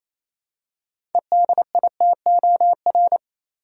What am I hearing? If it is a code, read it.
Morse code, heard as a keyed tone, about 28 wpm, 700 Hz. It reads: EDITOR
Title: Six Letter Words - 28wpm